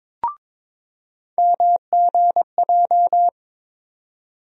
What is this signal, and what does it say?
Morse code, heard as a keyed tone, about 22 wpm, 700 Hz. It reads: MGJ
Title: USA Call Sign Postfixes - 3 Letters 22wpm